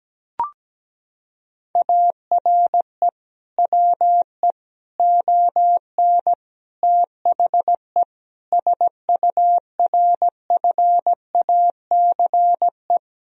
Morse code, heard as a keyed tone, about 17 wpm, 700 Hz. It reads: ARE WE ON THE SURFACE
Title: Sentences from Top 1000 Words - Review Part 3 - 17wpm